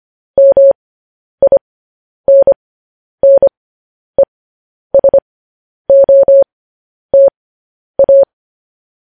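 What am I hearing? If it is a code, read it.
Morse code, heard as a keyed tone, about 25 wpm, spaced as for 10 wpm, 550 Hz. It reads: MINNESOTA
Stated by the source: Morse code practice sending news headlines at 10 wpm.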